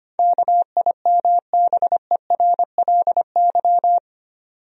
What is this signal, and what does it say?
Morse code, heard as a keyed tone, about 25 wpm, 700 Hz. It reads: KIMBERLY
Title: USA Common Names - 25wpm